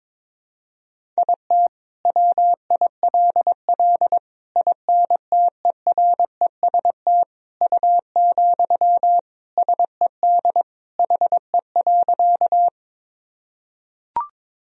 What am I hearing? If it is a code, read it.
Morse code, heard as a keyed tone, about 22 wpm, 700 Hz. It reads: IT WILL INTEREST U, SED HE.
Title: Sentences from Top 400 Words and Abbreviations 22wpm